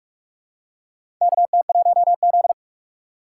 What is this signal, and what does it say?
Morse code, heard as a keyed tone, about 44 wpm, 700 Hz. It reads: KT1Z
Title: USA Call Signs - 4 Characters - 3x Character Spacing - 45wpm